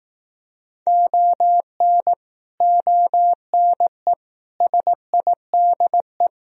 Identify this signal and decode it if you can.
Morse code, heard as a keyed tone, about 18 wpm, 700 Hz. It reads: ON ONE SIDE
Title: Speed-Racing - Sets of 5 Words 15wpm